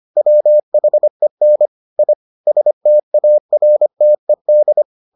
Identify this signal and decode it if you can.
Morse code, heard as a keyed tone, about 25 wpm, 600 Hz. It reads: WHEN I STARTED